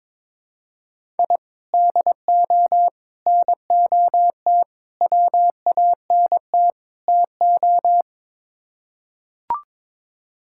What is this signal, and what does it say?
Morse code, heard as a keyed tone, about 22 wpm, 700 Hz. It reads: I DO NOT WANT TO
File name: practice / 22wpm / Build A Phrase - 5-Word Phrases - 22wpm